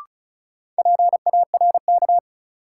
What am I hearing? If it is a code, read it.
Morse code, heard as a keyed tone, about 36 wpm, 700 Hz. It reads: PARK
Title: Sentences from Top 1300 Words - New Word Drill Alt - 35wpm